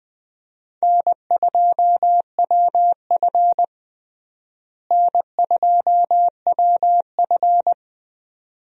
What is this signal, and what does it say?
Morse code, heard as a keyed tone, about 21 wpm, 700 Hz. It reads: N2WF N2WF
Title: USA Call Signs - 4 Characters - Repeated 2x - 20wpm